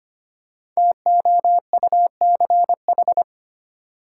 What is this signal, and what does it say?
Morse code, heard as a keyed tone, about 25 wpm, 700 Hz. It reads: TOUCH
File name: 5 Letter Words - First 3 Letters Given - 25wpm